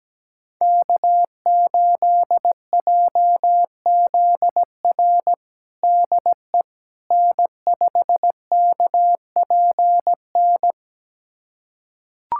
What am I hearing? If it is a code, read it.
Morse code, heard as a keyed tone, about 17 wpm, 700 Hz. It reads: K8JZR DE N5KPN
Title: USA Callsign DE Callsign - 5 Characters - 17wpm